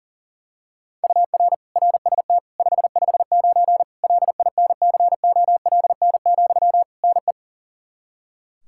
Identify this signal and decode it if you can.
Morse code, heard as a keyed tone, about 40 wpm, 700 Hz. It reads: UR RST 559 LINCOLN, NE
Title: Practice QSO - Exchange 1 - Beginning - 40wpm